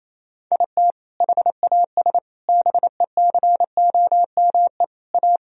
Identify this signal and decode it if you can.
Morse code, heard as a keyed tone, about 28 wpm, 700 Hz. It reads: IT HAS BECOME A
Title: Sentences from Top 1000 Words - Review Part 3 - 28wpm